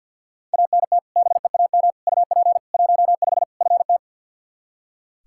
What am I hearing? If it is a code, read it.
Morse code, heard as a keyed tone, about 46 wpm, 700 Hz. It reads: ANT BEAM UP 15 FT
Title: Practice QSO - Exchange 2 - Info - 50wpm